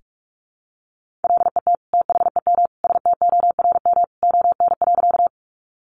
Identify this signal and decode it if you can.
Morse code, heard as a keyed tone, about 44 wpm, 700 Hz. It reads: LET THEM STORM ON.
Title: Sentences from Top 1500 Words 45wpm